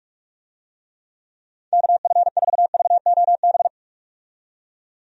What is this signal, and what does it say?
Morse code, heard as a keyed tone, about 46 wpm, 700 Hz. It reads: KF4VOB
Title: USA Call Signs - 6 Characters - 45wpm